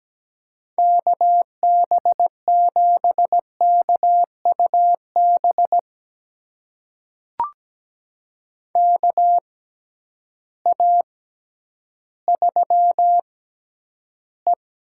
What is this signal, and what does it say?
Morse code, heard as a keyed tone, about 17 wpm, spaced as for 10 wpm, 700 Hz. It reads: KB7KUB KA3E
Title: USA Call Signs - 6x Character Spacing - 17wpm